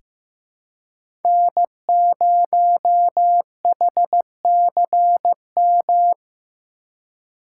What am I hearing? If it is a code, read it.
Morse code, heard as a keyed tone, about 16 wpm, 700 Hz. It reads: N0HCM
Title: USA Call Signs - 5 Characters - 3x Character Spacing - 15wpm